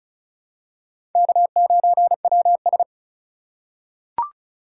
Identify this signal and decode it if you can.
Morse code, heard as a keyed tone, about 34 wpm, 700 Hz. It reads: K9WS
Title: USA Call Signs - 6x Character Spacing - 35wpm